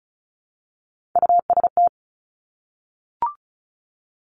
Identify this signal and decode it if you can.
Morse code, heard as a keyed tone, about 36 wpm, 700 Hz. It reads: UST
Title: Three Challenging Letter-Number Combinations 35wpm